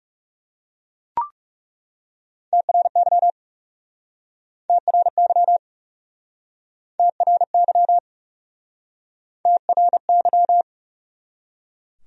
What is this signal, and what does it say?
Morse code, heard as a keyed tone, about 40 wpm, 700 Hz. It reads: TRY TRY TRY TRY